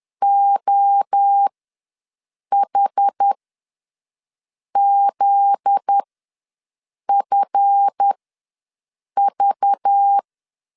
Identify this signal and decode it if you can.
Morse code, heard as a keyed tone, about 10 wpm, spaced as for 6 wpm, 800 Hz. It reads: OHZFV
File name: LEZIONI / 9 2 ws 10wpm 4130